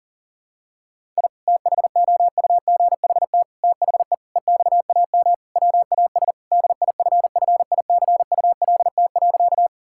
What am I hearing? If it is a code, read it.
Morse code, heard as a keyed tone, about 40 wpm, 700 Hz. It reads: I THOUGHT THE EXAM WAS DIFFICULT.
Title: Sentences from Top 1000 Words - Review Part 5 - 40wpm